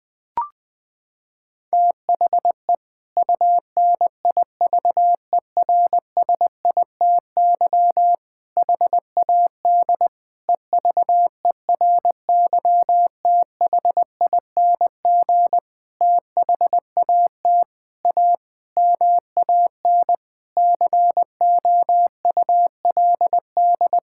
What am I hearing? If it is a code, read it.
Morse code, heard as a keyed tone, about 20 wpm, 700 Hz. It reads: THE UNIVERSITY HAD EVERYTHING THAT A MAN COULD
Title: Sentences from Top 1000 Words - Review Part 5 - 20wpm